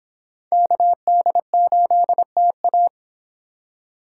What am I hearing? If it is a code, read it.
Morse code, heard as a keyed tone, about 25 wpm, 700 Hz. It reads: KD8TA